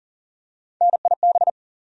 Morse code, heard as a keyed tone, about 40 wpm, 700 Hz. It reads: NIB
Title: USA Call Sign Postfixes - 3 Letters 40wpm